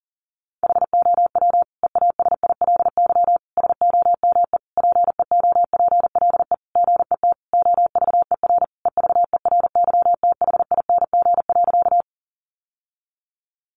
Morse code, heard as a keyed tone, about 40 wpm, 700 Hz. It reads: HOW EASILY SOME PEOPLE GET OVER EVERYTHING.
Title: Sentences from Top 900 Words 40wpm